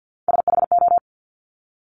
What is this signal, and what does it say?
Morse code, heard as a keyed tone, about 50 wpm, 700 Hz. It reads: SHO
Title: Top Three Letter Combinations in Words 50wpm